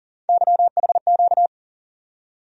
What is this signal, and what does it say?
Morse code, heard as a keyed tone, about 40 wpm, 700 Hz. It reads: YHQ
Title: USA Call Sign Postfixes - 3 Letters 40wpm